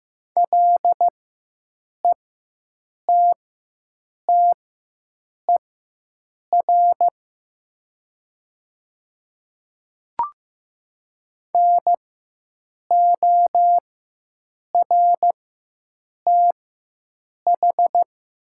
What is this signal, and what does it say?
Morse code, heard as a keyed tone, about 15 wpm, spaced as for 7 wpm, 700 Hz. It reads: LETTER NORTH
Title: 4x Character Spacing - Top 300 Words - 15wpm